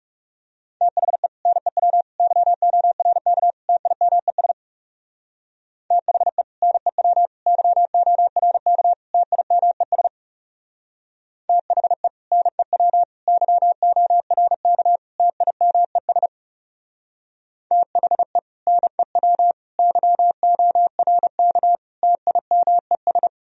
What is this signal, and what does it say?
Morse code, heard as a keyed tone, about 35 wpm, 700 Hz. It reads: THE NEW YORK TIMES THE NEW YORK TIMES THE NEW YORK TIMES THE NEW YORK TIMES
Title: Speed-Racing - Sets of 4 Words 30wpm